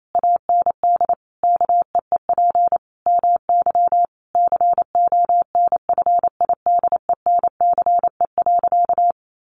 Morse code, heard as a keyed tone, about 28 wpm, 700 Hz. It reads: AND KEEP MY CONFIDENCE.
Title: Sentences from Top 1500 Words 28wpm